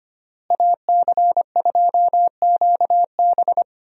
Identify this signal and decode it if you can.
Morse code, heard as a keyed tone, about 25 wpm, 700 Hz. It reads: AC2QB
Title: USA Call Signs - 5 Characters - Repeated 2x - 25wpm